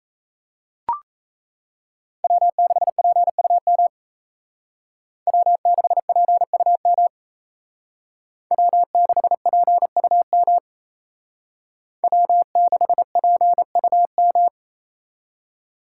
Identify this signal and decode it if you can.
Morse code, heard as a keyed tone, about 32 wpm, 700 Hz. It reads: W6PUM W6PUM W6PUM W6PUM
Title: Speed Racing - USA 1x3 Call Signs 28wpm